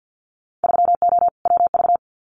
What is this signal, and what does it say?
Morse code, heard as a keyed tone, about 50 wpm, 700 Hz. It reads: 3O WV
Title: Winter Field Day Exchanges 50wpm